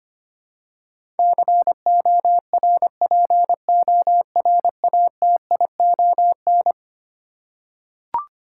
Morse code, heard as a keyed tone, about 25 wpm, 700 Hz. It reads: CORPORATION
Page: morse-code-ninja-practice-25wpm